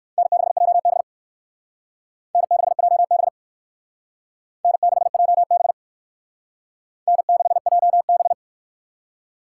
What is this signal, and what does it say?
Morse code, heard as a keyed tone, about 56 wpm, 700 Hz. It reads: N6JB N6JB N6JB N6JB